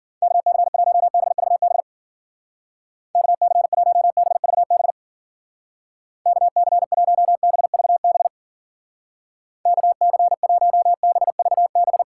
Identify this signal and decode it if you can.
Morse code, heard as a keyed tone, about 53 wpm, 700 Hz. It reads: KC1BVB KC1BVB KC1BVB KC1BVB